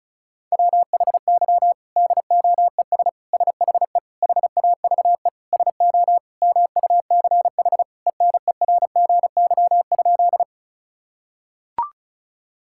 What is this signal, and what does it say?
Morse code, heard as a keyed tone, about 34 wpm, 700 Hz. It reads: WHY DOES SHE HAVE SO MUCH ENERGY?
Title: Sentences from Top 1000 Words - Review Part 6 - 35wpm